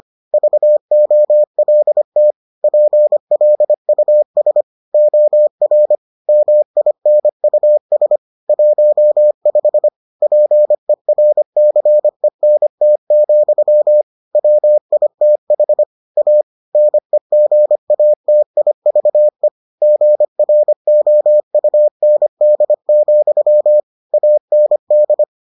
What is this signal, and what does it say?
Morse code, heard as a keyed tone, about 25 wpm, 600 Hz. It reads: VOLT PLUS OR MINUS 15 PERCENT, WITH A NEGATIVE GROUND, AND